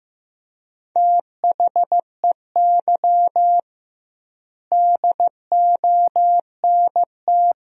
Morse code, heard as a keyed tone, about 15 wpm, 700 Hz. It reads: THEY DONT
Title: 2x Word-Spacing - Sentences from Top 300 Words - 15wpm